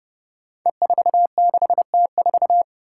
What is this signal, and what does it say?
Morse code, heard as a keyed tone, about 30 wpm, 700 Hz. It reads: E46T4
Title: Five Challenging Letter-Number Combinations 30wpm